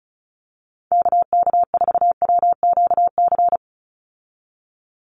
Morse code, heard as a keyed tone, about 34 wpm, 700 Hz. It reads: KK4WQC
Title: USA Call Signs 35wpm